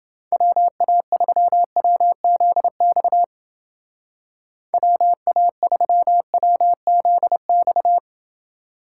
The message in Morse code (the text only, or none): WA3WZX WA3WZX